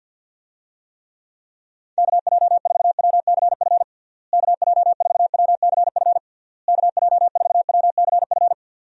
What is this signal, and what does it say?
Morse code, heard as a keyed tone, about 52 wpm, 700 Hz. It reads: KJ4WCF KJ4WCF KJ4WCF
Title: USA Call Signs - 6 Characters - Repeated 3x - 50wpm